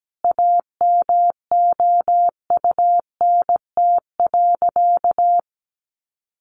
Morse code, heard as a keyed tone, about 17 wpm, 700 Hz. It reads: AMOUNT.
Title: Sentences from Top 700 Words 17wpm